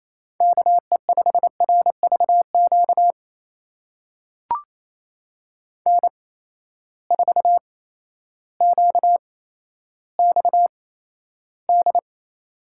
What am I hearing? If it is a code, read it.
Morse code, heard as a keyed tone, about 28 wpm, spaced as for 13 wpm, 700 Hz. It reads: KE5RVQ N4QXD
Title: USA Call Signs - 8x Character Spacing - 28wpm